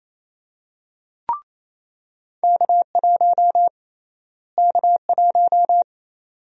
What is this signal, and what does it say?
Morse code, heard as a keyed tone, about 28 wpm, 700 Hz. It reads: K1 K1
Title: USA Call Sign Prefixes - Repeated 2x - 28wpm